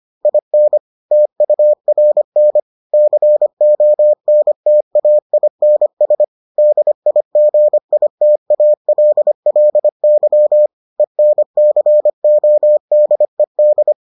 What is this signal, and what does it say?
Morse code, heard as a keyed tone, about 25 wpm, 600 Hz. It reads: IN TURN CONTAINS DIGITALLY ENCODED